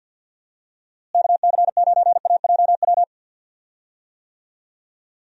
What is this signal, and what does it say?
Morse code, heard as a keyed tone, about 51 wpm, 700 Hz. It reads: KC9AJW
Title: USA Call Signs - 6 Characters - Repeated 3x - 50wpm